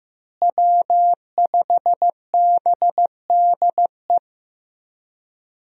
Morse code, heard as a keyed tone, about 15 wpm, 700 Hz. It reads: W5BDE
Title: USA Call Signs - 5 Characters - Repeated 2x - 15wpm